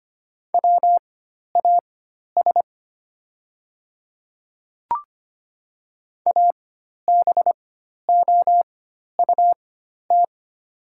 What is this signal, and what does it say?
Morse code, heard as a keyed tone, about 25 wpm, spaced as for 12 wpm, 700 Hz. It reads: WAS ABOUT
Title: 4x Character Spacing - Top 100 Words - 25wpm